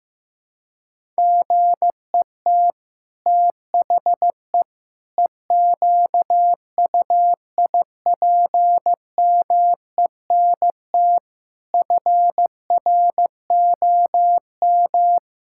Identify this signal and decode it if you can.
Morse code, heard as a keyed tone, about 15 wpm, 700 Hz. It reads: GET THE EQUIPMENT FROM
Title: Sentences from Top 1000 Words - Review Part 5 - 15wpm